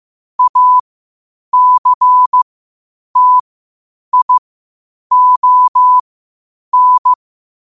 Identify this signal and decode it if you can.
Morse code, heard as a keyed tone, about 15 wpm, spaced as for 8 wpm, 1000 Hz. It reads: ACTION
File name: qotdfiles / 8wpm